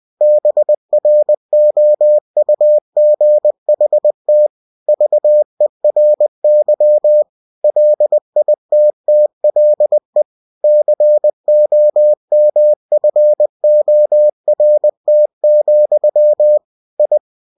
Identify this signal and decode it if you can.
Morse code, heard as a keyed tone, about 20 wpm, 600 Hz. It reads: BROUGHT VERY LITTLE COMFORT, I
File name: cw-books / winnie-the-pooh / winnie-20wpm-page-0024